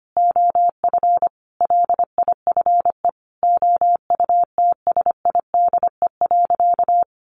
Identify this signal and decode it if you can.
Morse code, heard as a keyed tone, about 25 wpm, 700 Hz. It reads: OF LIFE OUTSIDE.
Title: Sentences from Top 1300 Words 25wpm